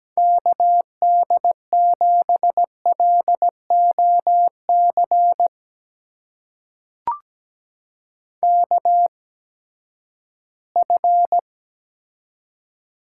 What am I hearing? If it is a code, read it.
Morse code, heard as a keyed tone, about 17 wpm, spaced as for 12 wpm, 700 Hz. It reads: KD7LOC KF
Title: USA Call Signs - 8x Character Spacing - 17wpm